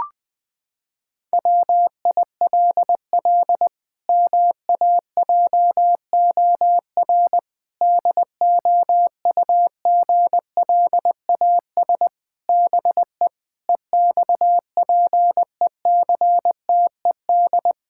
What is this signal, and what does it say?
Morse code, heard as a keyed tone, about 21 wpm, 700 Hz. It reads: WILL MAJOR DOUGLAS BE EXPECTED
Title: Pangram Challenge v2 - 20wpm